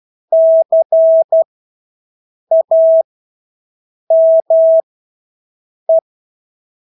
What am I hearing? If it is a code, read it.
Morse code, heard as a keyed tone, about 12 wpm, spaced as for 6 wpm, 650 Hz. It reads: CAME
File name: Engelske ord 12-6 wpm